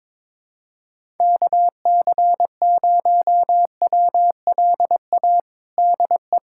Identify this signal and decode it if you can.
Morse code, heard as a keyed tone, about 22 wpm, 700 Hz. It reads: KC0WLA DE
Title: USA Callsign DE Callsign 22wpm